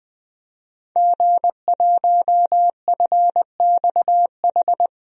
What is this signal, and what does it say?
Morse code, heard as a keyed tone, about 21 wpm, 700 Hz. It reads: G1FXH
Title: DXCC-100 Call Signs - 5 Characters - 20wpm